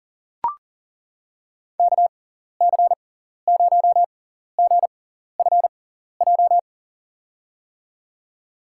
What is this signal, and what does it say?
Morse code, heard as a keyed tone, about 40 wpm, spaced as for 14 wpm, 700 Hz. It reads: KC0GFJ
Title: USA Call Signs - 6 Characters - 6x Character Spacing - 40wpm